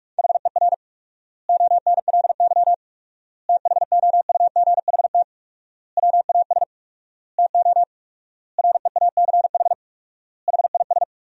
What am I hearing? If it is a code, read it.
Morse code, heard as a keyed tone, about 46 wpm, 700 Hz. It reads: HER ONLY THOUGHT WAS TO REACH HIS
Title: 4x Word-Spacing - Sentences from Top 400 Words - 45wpm